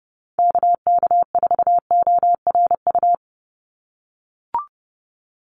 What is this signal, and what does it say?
Morse code, heard as a keyed tone, about 28 wpm, 700 Hz. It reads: KK4ORU
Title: USA Call Signs 30wpm